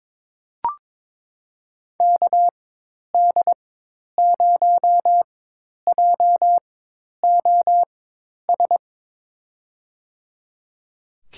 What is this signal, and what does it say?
Morse code, heard as a keyed tone, about 22 wpm, spaced as for 10 wpm, 700 Hz. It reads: KD0JOS
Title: USA Call Signs - 6 Characters - 4x Character Spacing - 22wpm